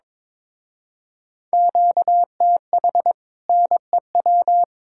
Morse code, heard as a keyed tone, about 22 wpm, 700 Hz. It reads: QTH NEW
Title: Speed-Racing Common QSO Phrases - 22wpm